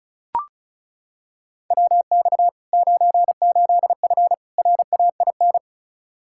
Morse code, heard as a keyed tone, about 34 wpm, 700 Hz. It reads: WX 98F RAIN
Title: Practice QSO - Exchange 2 - Info - 35wpm